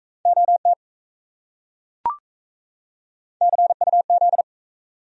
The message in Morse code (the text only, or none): OT CUZ